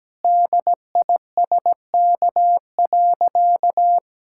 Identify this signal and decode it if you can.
Morse code, heard as a keyed tone, about 17 wpm, 700 Hz. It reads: DISK.
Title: Sentences from Top 1000 Words - Review Part 6 - 17wpm